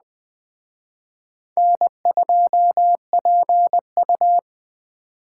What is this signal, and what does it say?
Morse code, heard as a keyed tone, about 20 wpm, 700 Hz. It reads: N2PU
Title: Speed Racing - USA Call Signs 17wpm